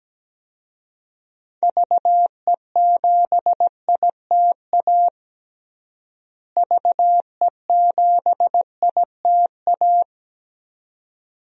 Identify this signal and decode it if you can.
Morse code, heard as a keyed tone, about 17 wpm, 700 Hz. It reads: VE7ITA VE7ITA
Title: DXCC-100 Call Signs - 6 Characters - Repeated 2x - 17wpm